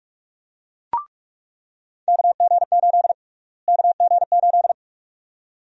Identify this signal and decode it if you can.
Morse code, heard as a keyed tone, about 43 wpm, 700 Hz. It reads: KG8 KG8
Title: USA Call Sign Prefixes - Repeated 2x - 45wpm